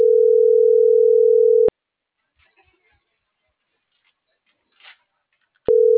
Ringback tone (North America), 440 + 480 Hz.
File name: CENTRAL NMP / out patricia.martineznmp - .19220